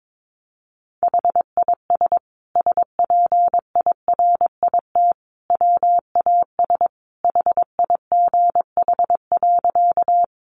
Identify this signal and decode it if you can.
Morse code, heard as a keyed tone, about 22 wpm, 700 Hz. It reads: HIS SPIRIT WAS HIGH.
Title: Sentences from Top 1400 Words 22wpm